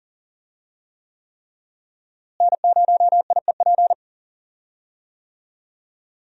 Morse code, heard as a keyed tone, about 39 wpm, 700 Hz. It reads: N0IEP
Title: USA Call Signs - 5 Characters - Repeated 3x - 40wpm